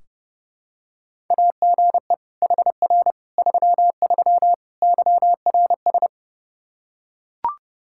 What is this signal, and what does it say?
Morse code, heard as a keyed tone, about 32 wpm, 700 Hz. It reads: AGE HR 33 YRS
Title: K1USN SST QSO Practice 30wpm